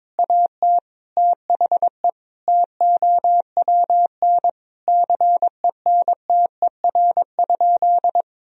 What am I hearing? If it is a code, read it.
Morse code, heard as a keyed tone, about 22 wpm, 700 Hz. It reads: AT THE TOWN CENTER?
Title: Sentences from Top 1000 Words - Review Part 2 - 22wpm